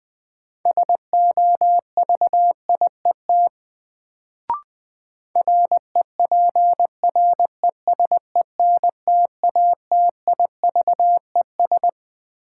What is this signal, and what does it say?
Morse code, heard as a keyed tone, about 21 wpm, 700 Hz. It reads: SOVIET REPRESENTATIVES
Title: Sets of 5 Words Encore - New Word Drill - 20wpm